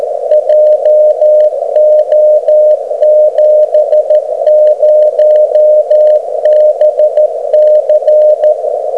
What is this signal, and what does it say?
Morse code, heard as a keyed tone, about 13 wpm, 610 Hz. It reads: JO70BC
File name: Beacon recordings / OK0EF